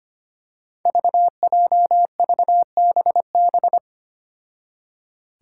Morse code, heard as a keyed tone, about 25 wpm, 700 Hz. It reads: VJVBB
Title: Five Challenging Letter-Number Combinations 25wpm